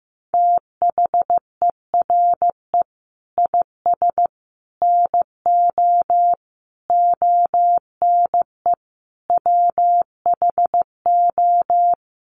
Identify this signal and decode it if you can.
Morse code, heard as a keyed tone, about 15 wpm, 700 Hz. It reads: THERE IS NO ONE WHO
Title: Sentences from Top 1100 Words 15wpm